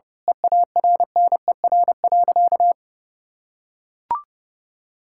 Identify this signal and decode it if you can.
Morse code, heard as a keyed tone, about 30 wpm, 700 Hz. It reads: EARNER.
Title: Pangram Challenge v2 - 30wpm